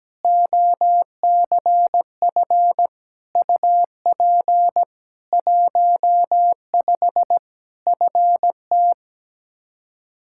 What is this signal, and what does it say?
Morse code, heard as a keyed tone, about 17 wpm, 700 Hz. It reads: OCF UP 15 FT
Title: Practice QSO - Exchange 2 - Info - 17wpm